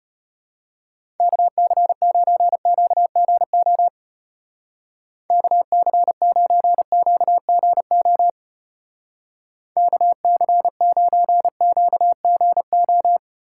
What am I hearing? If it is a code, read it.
Morse code, heard as a keyed tone, about 32 wpm, 700 Hz. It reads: KC9QGO KC9QGO KC9QGO